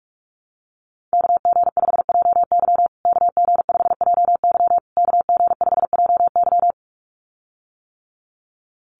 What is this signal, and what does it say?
Morse code, heard as a keyed tone, about 44 wpm, 700 Hz. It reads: KG5JY KG5JY KG5JY
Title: USA Call Signs Repeated 3x 45wpm